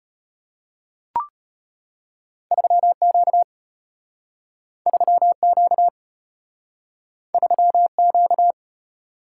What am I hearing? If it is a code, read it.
Morse code, heard as a keyed tone, about 34 wpm, 700 Hz. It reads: 3Q 3Q 3Q